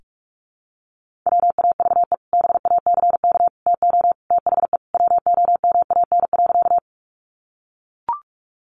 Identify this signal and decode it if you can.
Morse code, heard as a keyed tone, about 44 wpm, 700 Hz. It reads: WAVE BACK TO THE WOMAN.
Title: Sentences from Top 1400 Words 45wpm